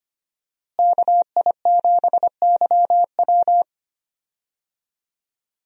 Morse code, heard as a keyed tone, about 25 wpm, 700 Hz. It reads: KI7YW
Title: USA Call Signs - 5 Characters - Repeated 3x - 25wpm